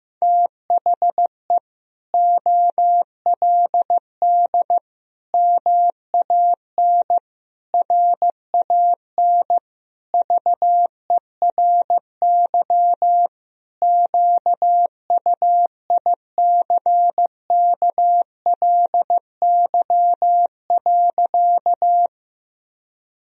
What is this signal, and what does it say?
Morse code, heard as a keyed tone, about 15 wpm, 700 Hz. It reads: THE OLD MAN RAN VERY QUICKLY.
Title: Sentences from Top 1000 Words - Review Part 3 - 15wpm